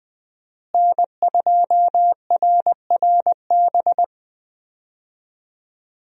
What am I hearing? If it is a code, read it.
Morse code, heard as a keyed tone, about 20 wpm, 700 Hz. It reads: N2RRB
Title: USA Call Signs - 5 Characters - Repeated 3x - 20wpm